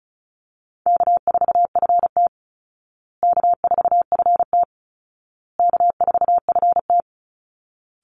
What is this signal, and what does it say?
Morse code, heard as a keyed tone, about 34 wpm, 700 Hz. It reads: K4FT K4FT K4FT
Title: Easy Call Signs Repeated 3x - 35wpm